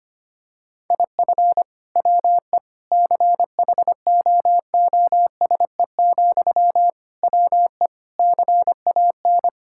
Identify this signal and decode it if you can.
Morse code, heard as a keyed tone, about 25 wpm, 700 Hz. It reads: IF WE CHOOSE, WE CAN
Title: Sentences from Top 1000 Words - Review Part 6 - 25wpm